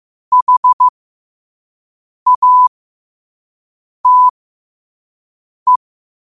Morse code, heard as a keyed tone, about 15 wpm, spaced as for 5 wpm, 1000 Hz. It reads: HATE
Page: New quotes every day in morse code at 5 Words per minute.